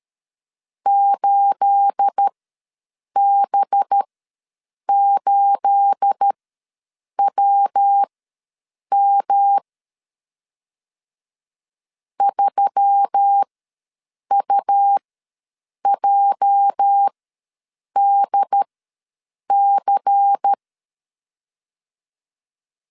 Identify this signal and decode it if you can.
Morse code, heard as a keyed tone, about 13 wpm, spaced as for 7 wpm, 800 Hz. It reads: 8B8WM 3UJDC
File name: LEZIONI / 9 28 ws 12wpm 3354